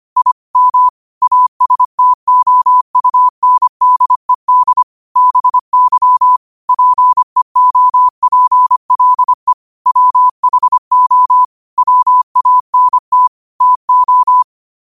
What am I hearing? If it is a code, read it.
Morse code, heard as a keyed tone, about 25 wpm, 1000 Hz. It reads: I M ASTOUNDED BY PEOPLE WHO WANT TO
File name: qotdfiles / 25wpm